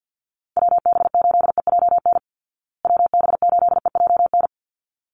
Morse code, heard as a keyed tone, about 48 wpm, 700 Hz. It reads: WB8EJN WB8EJN
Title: Call Signs Repeated 3x - 50wpm